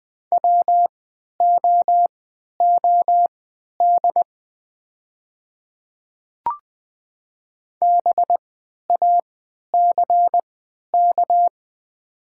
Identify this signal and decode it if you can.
Morse code, heard as a keyed tone, about 20 wpm, spaced as for 11 wpm, 700 Hz. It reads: WOOD BACK
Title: 3x Character Spacing - Top 300 Words - 20wpm